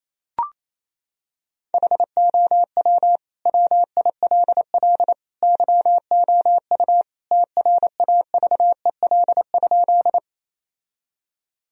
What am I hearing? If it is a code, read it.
Morse code, heard as a keyed tone, about 28 wpm, 700 Hz. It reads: HOW WILL YOU TRAVEL?
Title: Sentences from Top 1000 Words - Review Part 2 - 28wpm